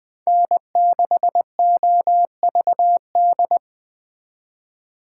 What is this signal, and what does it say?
Morse code, heard as a keyed tone, about 20 wpm, 700 Hz. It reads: N6OVD